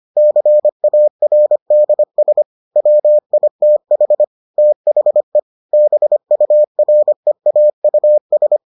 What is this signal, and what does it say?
Morse code, heard as a keyed tone, about 25 wpm, 600 Hz. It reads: CARDS WITH THE BUREAUS